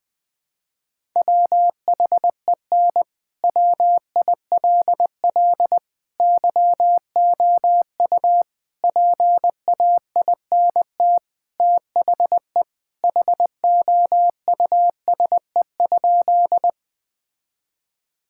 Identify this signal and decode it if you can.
Morse code, heard as a keyed tone, about 20 wpm, 700 Hz. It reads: WHEN WILL YOU PAINT THE HOUSE?
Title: Sentences from Top 1000 Words - Review Part 6 - 20wpm